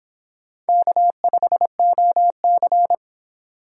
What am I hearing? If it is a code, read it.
Morse code, heard as a keyed tone, about 25 wpm, 700 Hz. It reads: K5OC